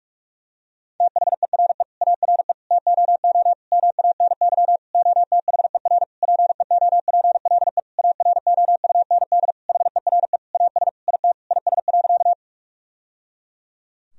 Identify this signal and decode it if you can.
Morse code, heard as a keyed tone, about 46 wpm, 700 Hz. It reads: THERE ARE TOO MANY OTHER PEOPLE AROUND HERE AS IT IS.
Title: Sentences from Top 1000 Words - Review Part 3 - 45wpm